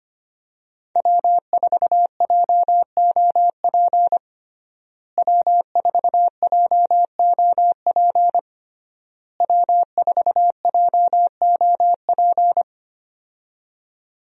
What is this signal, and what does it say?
Morse code, heard as a keyed tone, about 25 wpm, 700 Hz. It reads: W4JOP W4JOP W4JOP
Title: USA Call Signs - 5 Characters - Repeated 3x - 25wpm